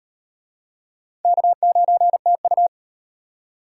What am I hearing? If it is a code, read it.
Morse code, heard as a keyed tone, about 36 wpm, 700 Hz. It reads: K9TU